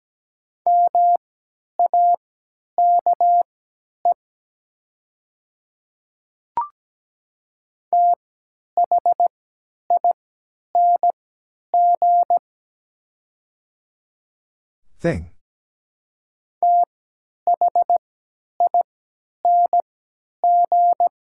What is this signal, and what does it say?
Morse code, heard as a keyed tone, about 17 wpm, spaced as for 10 wpm, 700 Hz. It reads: MAKE THING THING
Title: 3x Character Spacing - Top 100 Words - 17wpm